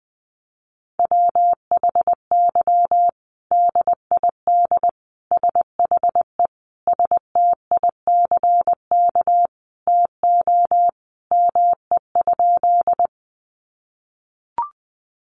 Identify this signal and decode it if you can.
Morse code, heard as a keyed tone, about 20 wpm, 700 Hz. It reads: WHY DID SHE STICK TO ME?
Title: Sentences from Top 1200 Words 20wpm